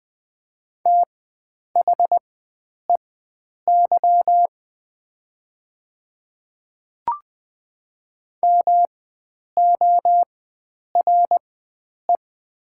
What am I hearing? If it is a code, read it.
Morse code, heard as a keyed tone, about 20 wpm, spaced as for 9 wpm, 700 Hz. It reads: THEY MORE
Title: 4x Character Spacing - Top 100 Words - 20wpm